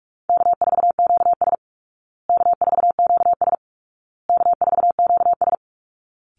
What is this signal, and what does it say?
Morse code, heard as a keyed tone, about 44 wpm, 700 Hz. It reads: K4QS K4QS K4QS
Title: Call Signs Repeated 2x - 45wpm